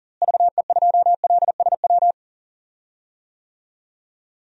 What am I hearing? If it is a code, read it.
Morse code, heard as a keyed tone, about 39 wpm, 700 Hz. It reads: VE2LSW
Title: DXCC-100 Call Signs - 6 Characters - Repeated 2x - 40wpm